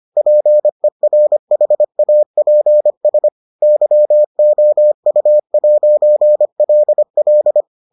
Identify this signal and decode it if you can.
Morse code, heard as a keyed tone, about 25 wpm, 600 Hz. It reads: PERHAPS YOU'LL